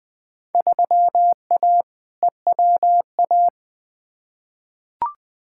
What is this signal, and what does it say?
Morse code, heard as a keyed tone, about 21 wpm, 700 Hz. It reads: 3A EWA
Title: Field Day Exchanges Encore - 20wpm